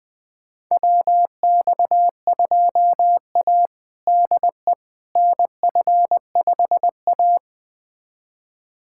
Morse code, heard as a keyed tone, about 20 wpm, 700 Hz. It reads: WX2A DE NF5A
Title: USA Callsign DE Callsign - 4 Characters - 20wpm